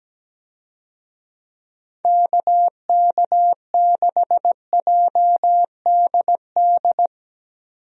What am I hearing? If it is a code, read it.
Morse code, heard as a keyed tone, about 17 wpm, 700 Hz. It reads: KK6JDD
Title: USA Call Signs - 6 Characters - 17wpm